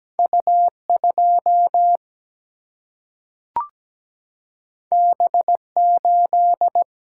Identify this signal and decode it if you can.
Morse code, heard as a keyed tone, about 17 wpm, 700 Hz. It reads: U 2 B 8
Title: Single-Letter Single-Number 17wpm